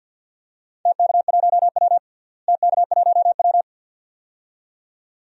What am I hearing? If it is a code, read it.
Morse code, heard as a keyed tone, about 48 wpm, 700 Hz. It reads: TK1W TK1W
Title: DXCC-100 Call Signs - 4 Characters - Repeated 2x - 50wpm